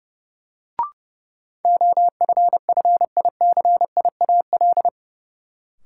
Morse code, heard as a keyed tone, about 32 wpm, 700 Hz. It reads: OFFICIAL
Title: Sentences from Top 1000 Words - New Word Drill - 30wpm